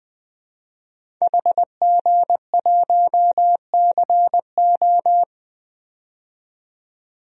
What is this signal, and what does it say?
Morse code, heard as a keyed tone, about 21 wpm, 700 Hz. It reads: HG1CO
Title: DXCC-100 Call Signs - 5 Characters - 20wpm